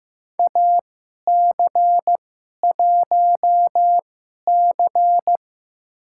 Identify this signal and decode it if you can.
Morse code, heard as a keyed tone, about 15 wpm, spaced as for 11 wpm, 700 Hz. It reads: AC1C
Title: USA Call Signs - 4 Characters - 2x Character Spacing - 15wpm